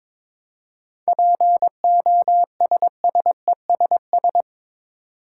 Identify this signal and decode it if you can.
Morse code, heard as a keyed tone, about 22 wpm, 700 Hz. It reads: POSSESS
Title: Seven Letter Words - 22wpm